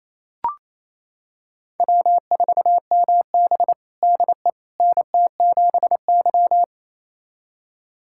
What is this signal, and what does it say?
Morse code, heard as a keyed tone, about 27 wpm, 700 Hz. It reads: W4MB DE NT7Y
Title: USA Callsign DE Callsign - 4 Characters - 28wpm